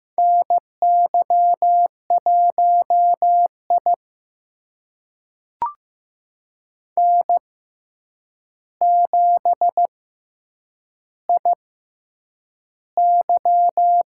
Morse code, heard as a keyed tone, about 15 wpm, spaced as for 8 wpm, 700 Hz. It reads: NY1I N7IY
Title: USA Call Signs - 4 Characters - 6x Character Spacing - 15wpm